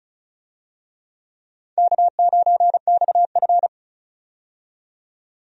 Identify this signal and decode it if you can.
Morse code, heard as a keyed tone, about 34 wpm, 700 Hz. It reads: K9XF
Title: USA Call Signs - 4 Characters - 35wpm